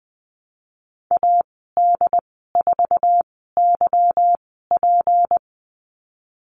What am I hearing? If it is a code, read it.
Morse code, heard as a keyed tone, about 20 wpm, spaced as for 14 wpm, 700 Hz. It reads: AD4YP
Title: 173 Lesson 28g - Number 9 - Call Signs 20wpm